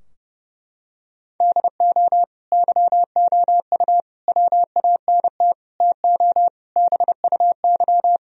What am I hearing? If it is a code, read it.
Morse code, heard as a keyed tone, about 30 wpm, 700 Hz. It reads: DO YOU WANT TO BUY
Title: Sentences from Top 1000 Words - Review Part 4 - 30wpm